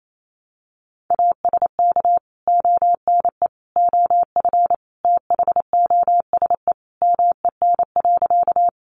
Morse code, heard as a keyed tone, about 28 wpm, 700 Hz. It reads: ASK ONE OF THOSE MEN.
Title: Sentences from Top 300 Words 28wpm